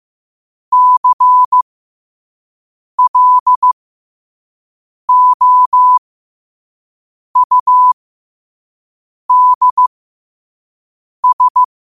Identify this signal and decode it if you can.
Morse code, heard as a keyed tone, about 15 wpm, spaced as for 5 wpm, 1000 Hz. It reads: CLOUDS
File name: qotdfiles / 5wpm